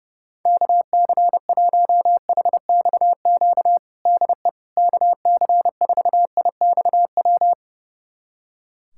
Morse code, heard as a keyed tone, about 32 wpm, 700 Hz. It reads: KC1HXQ DE KC4IXW
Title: USA Callsign DE Callsign - 6 Characters - 30wpm